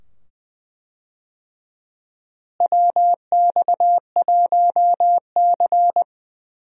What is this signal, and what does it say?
Morse code, heard as a keyed tone, about 20 wpm, 700 Hz. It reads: WX1C
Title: USA Call Signs - 4 Characters - Repeated 2x - 20wpm